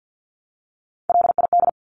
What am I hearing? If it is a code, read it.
Morse code, heard as a keyed tone, about 50 wpm, 700 Hz. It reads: LID
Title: Abbreviations 50wpm